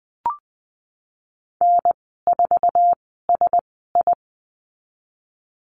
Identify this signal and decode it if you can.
Morse code, heard as a keyed tone, about 20 wpm, spaced as for 15 wpm, 700 Hz. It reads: N4SI
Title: 055 Lesson 10g - Letter r - Call Signs 20wpm